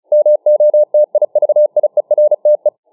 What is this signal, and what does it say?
Morse code, heard as a keyed tone, about 34 wpm, 600 Hz. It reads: MOTIVIERTE